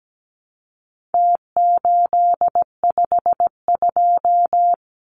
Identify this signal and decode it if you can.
Morse code, heard as a keyed tone, about 17 wpm, 700 Hz. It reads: T852
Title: Numbers - Contest Style - 4-Digit 17wpm